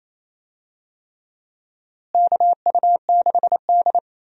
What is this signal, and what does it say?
Morse code, heard as a keyed tone, about 28 wpm, 700 Hz. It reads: KU6D
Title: USA Call Signs - 4 Characters - Repeated 3x - 28wpm